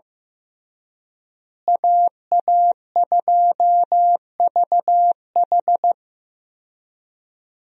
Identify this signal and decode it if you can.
Morse code, heard as a keyed tone, about 15 wpm, 700 Hz. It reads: AA2VH
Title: USA Call Signs - 5 Characters - Repeated 3x - 15wpm